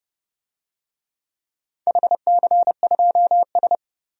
Morse code, heard as a keyed tone, about 30 wpm, 700 Hz. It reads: HC2S
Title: DXCC-100 Call Signs - 4 Characters - 30wpm